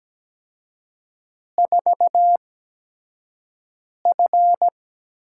Morse code, heard as a keyed tone, about 17 wpm, spaced as for 5 wpm, 700 Hz. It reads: 4F
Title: USA Call Signs - 5 Characters - 8x Character Spacing - 17wpm